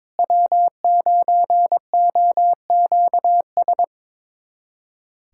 Morse code, heard as a keyed tone, about 22 wpm, 700 Hz. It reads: W9OQS
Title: USA Callsign DE Callsign - 5 Characters - 22wpm